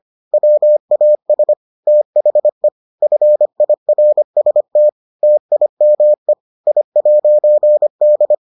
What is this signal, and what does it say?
Morse code, heard as a keyed tone, about 25 wpm, 600 Hz. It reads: WAS THE FIRST TIME I'D